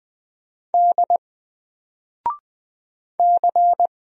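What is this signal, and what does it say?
Morse code, heard as a keyed tone, about 20 wpm, 700 Hz. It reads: D C
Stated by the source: Single letter and number Morse drill at 20 wpm, from N4LCD.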